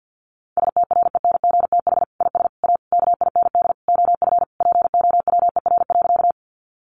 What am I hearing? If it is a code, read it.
Morse code, heard as a keyed tone, about 50 wpm, 700 Hz. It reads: STRENGTH IS A KIND OF POWER.
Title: Sentences from Top 1300 Words 50wpm